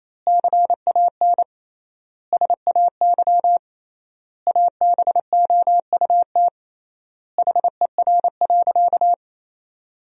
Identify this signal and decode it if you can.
Morse code, heard as a keyed tone, about 28 wpm, 700 Hz. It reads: CAN SAY ABOUT HER.
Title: 3x Word-Spacing - Sentences from Top 200 Words - 28wpm